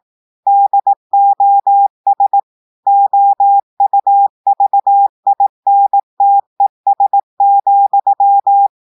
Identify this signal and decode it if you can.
Morse code, heard as a keyed tone, about 18 wpm, 800 Hz. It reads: DOS OUVINTES,
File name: audios / portugais / portugais texte en clair 2 vitesse 18 mots minutes